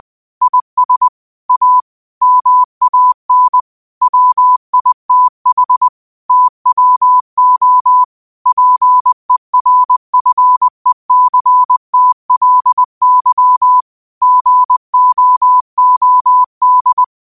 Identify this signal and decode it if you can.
Morse code, heard as a keyed tone, about 20 wpm, 1000 Hz. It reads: IS A MAN WITH TWO PERFECTLY GOOD
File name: qotdfiles / 20wpm